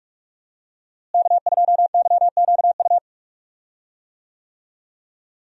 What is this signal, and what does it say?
Morse code, heard as a keyed tone, about 46 wpm, 700 Hz. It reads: K2YQU
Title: USA Call Signs - 5 Characters - Repeated 3x - 45wpm